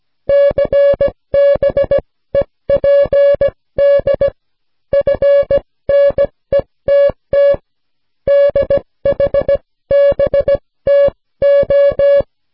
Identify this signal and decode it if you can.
Morse code, heard as a keyed tone, about 16 wpm, spaced as for 14 wpm, 560 Hz. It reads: CBEPD FNETT DHBTO